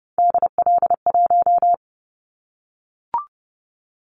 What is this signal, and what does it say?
Morse code, heard as a keyed tone, about 31 wpm, 700 Hz. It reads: DL1
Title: Callsign Prefixes for Top 25 DXCC - 30wpm